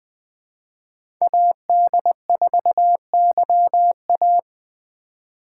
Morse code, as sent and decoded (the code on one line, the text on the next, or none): .- -.. ....- -.-- .-
AD4YA